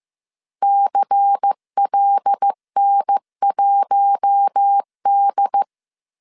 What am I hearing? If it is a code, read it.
Morse code, heard as a keyed tone, about 15 wpm, 800 Hz. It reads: CLN1D